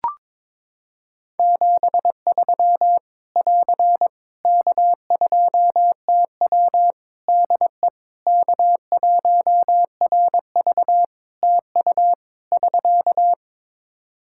Morse code, heard as a keyed tone, about 22 wpm, 700 Hz. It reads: 73 <AR> K2TW DE K1RV TU <SK>
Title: K1USN SST QSO Practice 22wpm